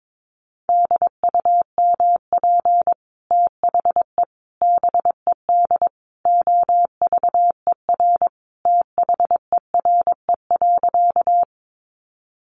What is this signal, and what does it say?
Morse code, heard as a keyed tone, about 22 wpm, 700 Hz. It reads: DUMP THE BED OVER THERE.
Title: Sentences from Top 1300 Words 22wpm